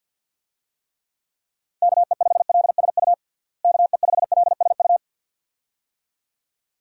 Morse code, heard as a keyed tone, about 51 wpm, 700 Hz. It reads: KE5LSU KE5LSU
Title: USA Call Signs - 6 Characters - Repeated 2x - 50wpm